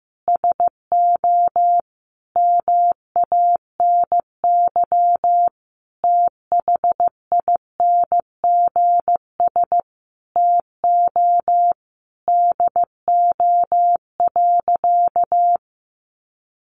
Morse code, heard as a keyed tone, about 15 wpm, 700 Hz. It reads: SO MANY THINGS TO DO.
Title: Additional Easy Phrases 15wpm